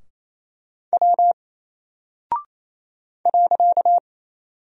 Morse code, heard as a keyed tone, about 27 wpm, 700 Hz. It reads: W .
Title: Single Character (Letter:Number:Punctuation:Prosign) Alt 28wpm